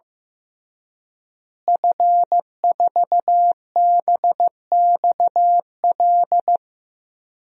Morse code, heard as a keyed tone, about 15 wpm, 700 Hz. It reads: F4BXL
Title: DXCC-100 Call Signs - 5 Characters - Repeated 3x - 15wpm